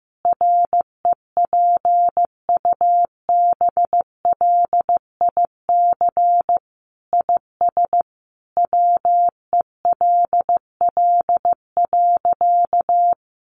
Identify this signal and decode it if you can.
Morse code, heard as a keyed tone, about 15 wpm, 700 Hz. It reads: REPUBLIC IS WELL.
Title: Sentences from Top 1500 Words 15wpm